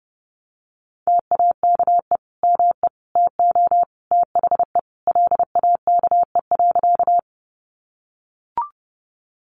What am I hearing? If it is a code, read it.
Morse code, heard as a keyed tone, about 30 wpm, 700 Hz. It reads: TAKE ME TO THE LAKE.
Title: Sentences from Top 1100 Words 30wpm